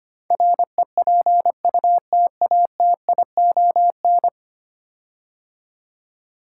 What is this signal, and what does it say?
Morse code, heard as a keyed tone, about 25 wpm, 690 Hz. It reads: REPUTATION
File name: 10 Letter Words - First 3 Letters Given - 25wpm